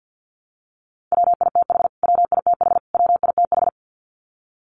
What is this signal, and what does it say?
Morse code, heard as a keyed tone, about 50 wpm, 700 Hz. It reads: WITH WITH WITH
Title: Top 500 Words Repeated 50wpm